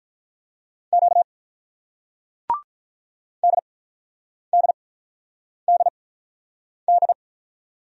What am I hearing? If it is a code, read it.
Morse code, heard as a keyed tone, about 45 wpm, spaced as for 9 wpm, 700 Hz. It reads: Q DDDD